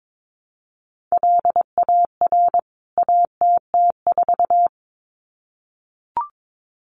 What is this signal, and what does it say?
Morse code, heard as a keyed tone, about 22 wpm, 700 Hz. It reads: LAR ATT4
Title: CWT Exchange 22wpm